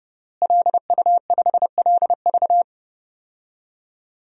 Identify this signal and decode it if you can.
Morse code, heard as a keyed tone, about 32 wpm, 700 Hz. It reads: LU5LV
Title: DXCC-100 Call Signs - 5 Characters - Repeated 3x - 30wpm